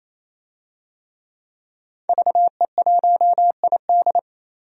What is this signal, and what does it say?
Morse code, heard as a keyed tone, about 28 wpm, 700 Hz. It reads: VE1ID
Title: DXCC-100 Call Signs - 5 Characters - 28wpm